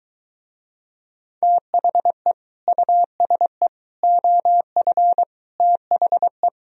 Morse code, heard as a keyed tone, about 23 wpm, 700 Hz. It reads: THE USE OF THE